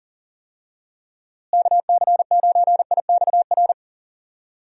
Morse code, heard as a keyed tone, about 38 wpm, 700 Hz. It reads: KC9IXR
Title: USA Call Signs - 6 Characters - Repeated 3x - 40wpm